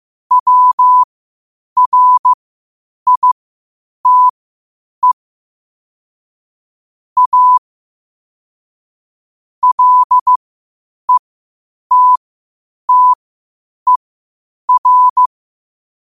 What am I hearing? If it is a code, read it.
Morse code, heard as a keyed tone, about 15 wpm, spaced as for 8 wpm, 1000 Hz. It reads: WRITE A LETTER
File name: qotdfiles / 8wpm